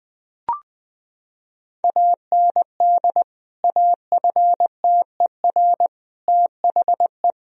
Morse code, heard as a keyed tone, about 20 wpm, 700 Hz. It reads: AND AFTER THE
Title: Sentences from Top 1000 Words - Review Part 2 - 20wpm